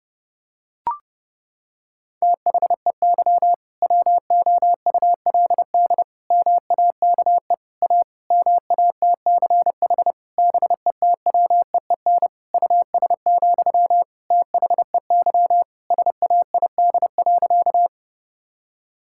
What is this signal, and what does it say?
Morse code, heard as a keyed tone, about 30 wpm, 700 Hz. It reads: THEY WOULD MAKE A MATCH BETWEEN US, THEY SAID.
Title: Sentences from Top 1600 Words 30wpm